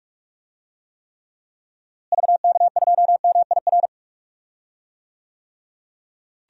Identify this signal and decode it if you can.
Morse code, heard as a keyed tone, about 46 wpm, 700 Hz. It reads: VK2MIR
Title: DXCC-100 Call Signs - 6 Characters - Repeated 2x - 45wpm